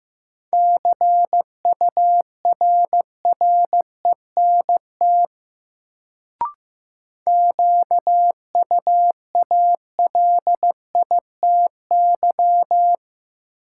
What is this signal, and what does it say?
Morse code, heard as a keyed tone, about 16 wpm, 700 Hz. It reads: CURRENT QUALITY
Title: Sentences from Top 1000 Words - Review Part 4 - New Word Drill - 15wpm